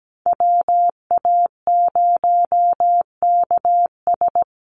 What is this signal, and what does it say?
Morse code, heard as a keyed tone, about 17 wpm, 700 Hz. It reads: WA0KS
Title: USA Call Signs Repeated 3x 17wpm